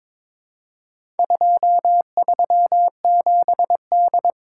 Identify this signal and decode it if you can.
Morse code, heard as a keyed tone, about 21 wpm, 700 Hz. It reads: 237D
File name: practice / 22wpm / Four Letter-Number Combinations - 22wpm